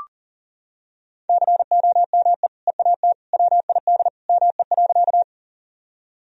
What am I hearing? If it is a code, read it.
Morse code, heard as a keyed tone, about 37 wpm, 700 Hz. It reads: COME EAT WID ME.
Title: Sentences from Top 300 Words and Abbreviations 40wpm